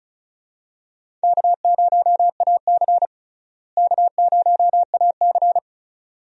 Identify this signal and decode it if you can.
Morse code, heard as a keyed tone, about 36 wpm, 700 Hz. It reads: K0AC K0AC
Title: USA Call Signs - 4 Characters - Repeated 2x - 35wpm